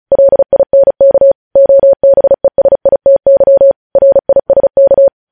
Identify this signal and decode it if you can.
Morse code, heard as a keyed tone, about 36 wpm, 550 Hz. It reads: LINK OBESITY RISK
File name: podcast / News headlines-1x-35x35